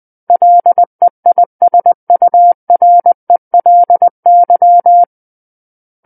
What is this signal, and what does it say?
Morse code, heard as a keyed tone, about 20 wpm, 700 Hz. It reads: LEISURELY
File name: 20 WPM